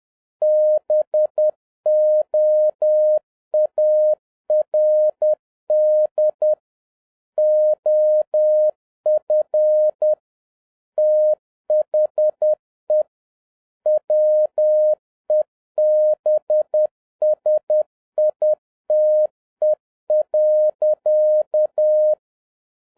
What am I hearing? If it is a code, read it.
Morse code, heard as a keyed tone, about 10 wpm, 610 Hz. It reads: BOARD OF THE WEBSITE.
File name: 2011-07-mp3 / n36 THE 3.11 2 10wpm